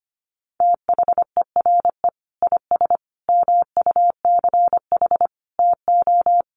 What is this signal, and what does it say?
Morse code, heard as a keyed tone, about 25 wpm, 700 Hz. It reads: THERE IS MUCH TO
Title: Sentences from Top 1300 Words 25wpm